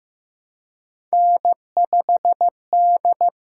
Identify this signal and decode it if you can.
Morse code, heard as a keyed tone, about 15 wpm, 700 Hz. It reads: N5D
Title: USA Call Signs - 8x Character Spacing - 15wpm